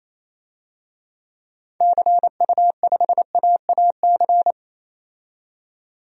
Morse code, heard as a keyed tone, about 28 wpm, 700 Hz. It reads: CU5AAC
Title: DXCC-100 Call Signs - 6 Characters - 28wpm